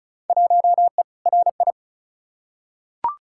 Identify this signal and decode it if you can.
Morse code, heard as a keyed tone, about 35 wpm, 700 Hz. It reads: 1E RI
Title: Field Day Exchanges Encore - 35wpm